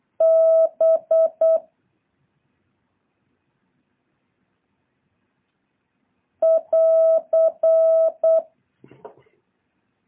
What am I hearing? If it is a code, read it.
Morse code, heard as a keyed tone, about 8 wpm, 650 Hz. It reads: B <AR>
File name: jh8cbh / 2023cw